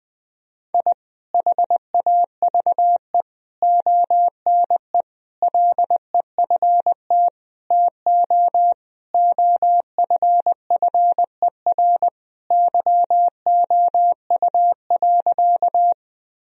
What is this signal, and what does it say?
Morse code, heard as a keyed tone, about 20 wpm, 700 Hz. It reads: I HAVE ONE LEFT TO OFFER YOU.
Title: Sentences from Top 1000 Words - Review Part 4 - 20wpm